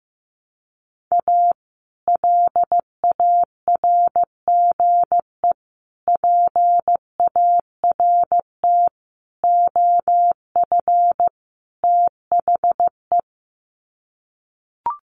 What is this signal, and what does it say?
Morse code, heard as a keyed tone, about 15 wpm, 700 Hz. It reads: A LARGE PART OF THE
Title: Sets of 5 Words Encore2 15wpm